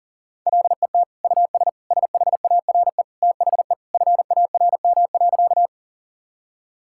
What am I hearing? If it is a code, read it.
Morse code, heard as a keyed tone, about 40 wpm, 700 Hz. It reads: LET US SHARE THE FARM.
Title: Sentences from Top 1000 Words - Review Part 6 - 40wpm